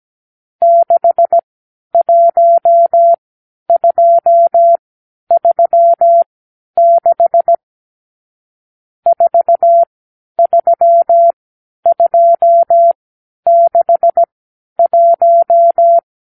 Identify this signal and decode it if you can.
Morse code, heard as a keyed tone, about 17 wpm, spaced as for 10 wpm, 680 Hz. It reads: 61236 43261
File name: k7qo / Lesson 52 (8)